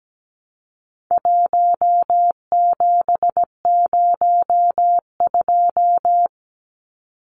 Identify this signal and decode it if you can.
Morse code, heard as a keyed tone, about 17 wpm, 700 Hz. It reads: 1702
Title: Numbers - 4-Digit 17wpm